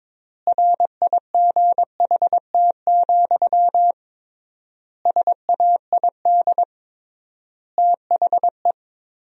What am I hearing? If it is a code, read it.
Morse code, heard as a keyed tone, about 22 wpm, 700 Hz. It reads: RIGHT, SAID THE
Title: 3x Word-Spacing - Sentences from Top 200 Words - 22wpm